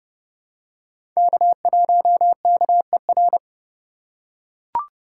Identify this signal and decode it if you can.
Morse code, heard as a keyed tone, about 30 wpm, 700 Hz. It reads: K1KER
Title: USA Call Signs - 5 Characters - 2x Character Spacing - 30wpm